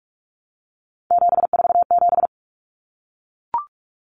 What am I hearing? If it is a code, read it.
Morse code, heard as a keyed tone, about 44 wpm, 700 Hz. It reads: 747
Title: Numbers - 3-Digit 45wpm